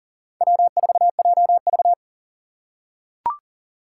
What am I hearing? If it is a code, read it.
Morse code, heard as a keyed tone, about 40 wpm, 700 Hz. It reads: W4JV
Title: USA Call Signs - 4 Characters - 4x Character Spacing - 40wpm